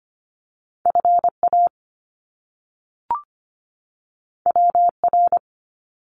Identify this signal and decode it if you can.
Morse code, heard as a keyed tone, about 25 wpm, 700 Hz. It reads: FA WR
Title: Top Two Letter Combinations in Words 25wpm